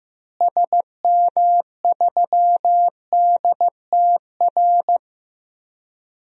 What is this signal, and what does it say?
Morse code, heard as a keyed tone, about 15 wpm, 700 Hz. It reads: SM3DTR
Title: DXCC-100 Call Signs - 6 Characters - Repeated 3x - 15wpm